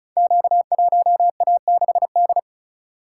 Morse code, heard as a keyed tone, about 36 wpm, 700 Hz. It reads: Q1A6D
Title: Code Groups - 5 Letter:Numbers - 35wpm